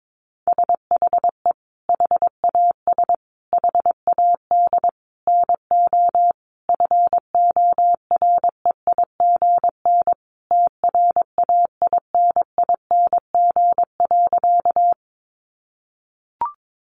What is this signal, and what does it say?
Morse code, heard as a keyed tone, about 22 wpm, 700 Hz. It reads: SHE HAS HAD NO FOREIGN TRAINING.
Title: Sentences from Top 1400 Words 22wpm